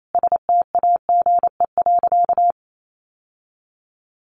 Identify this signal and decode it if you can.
Morse code, heard as a keyed tone, about 28 wpm, 700 Hz. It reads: STAGE.
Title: Sentences from Top 1200 Words 28wpm